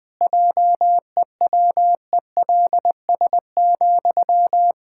Morse code, heard as a keyed tone, about 20 wpm, 700 Hz. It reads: JEWELS,
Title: Pangram Challenge v2 - 20wpm